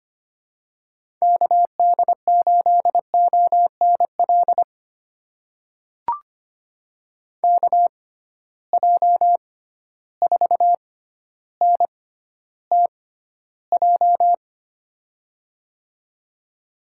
Morse code, heard as a keyed tone, about 25 wpm, spaced as for 13 wpm, 700 Hz. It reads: KD8ONL KJ4NTJ
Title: USA Call Signs - 6 Characters - 6x Character Spacing - 25wpm